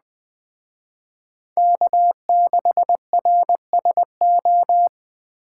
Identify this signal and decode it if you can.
Morse code, heard as a keyed tone, about 20 wpm, 700 Hz. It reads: K6RSO
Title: Speed Racing - USA 1x3 Call Signs 17wpm